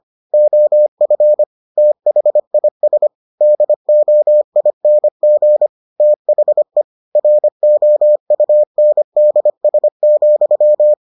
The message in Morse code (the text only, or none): OF THIS DOING THE ROUNDS,